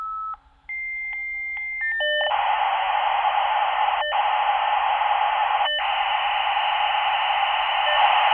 Dial-up modem handshake.